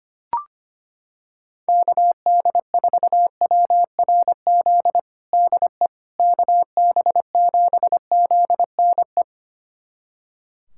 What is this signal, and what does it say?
Morse code, heard as a keyed tone, about 25 wpm, 700 Hz. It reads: KD4WRZ DE KB7ZNE
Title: USA Callsign DE Callsign - 6 Characters - 25wpm